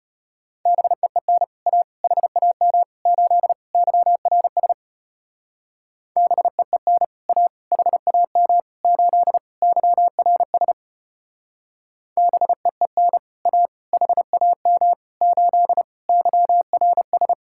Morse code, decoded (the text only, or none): BEEN A HAM 8 YRS BEEN A HAM 8 YRS BEEN A HAM 8 YRS